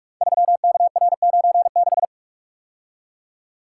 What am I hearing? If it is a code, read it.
Morse code, heard as a keyed tone, about 44 wpm, 700 Hz. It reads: 3KR96
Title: Code Groups - 5 Letter:Numbers - 45wpm